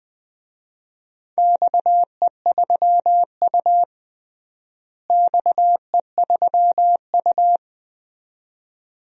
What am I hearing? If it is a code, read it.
Morse code, heard as a keyed tone, about 19 wpm, 700 Hz. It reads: XE3U XE3U
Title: DXCC-100 Call Signs - 4 Characters - Repeated 2x - 20wpm